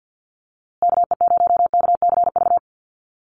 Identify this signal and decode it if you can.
Morse code, heard as a keyed tone, about 48 wpm, 700 Hz. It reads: KE0KCV
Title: USA Call Signs Repeated 3x 50wpm